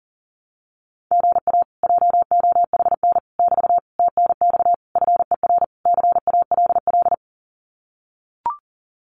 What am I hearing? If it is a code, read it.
Morse code, heard as a keyed tone, about 40 wpm, 700 Hz. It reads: GA JOHN <BT> TNX FER CALL
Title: Common QSO Phrases Encore 40wpm